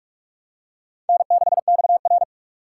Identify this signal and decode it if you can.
Morse code, heard as a keyed tone, about 44 wpm, 700 Hz. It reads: N6XR
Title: USA Call Signs - 4 Characters - 6x Character Spacing - 45wpm